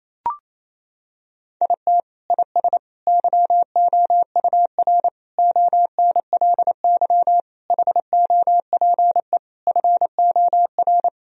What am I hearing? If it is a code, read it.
Morse code, heard as a keyed tone, about 28 wpm, 700 Hz. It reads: IT IS YOUR ONLY HOPE FOR
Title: Sentences from Top 1000 Words - Review Part 4 - 28wpm